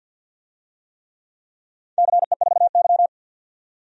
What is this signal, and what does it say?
Morse code, heard as a keyed tone, about 52 wpm, 700 Hz. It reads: CE4Y
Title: DXCC-100 Call Signs - 4 Characters - 50wpm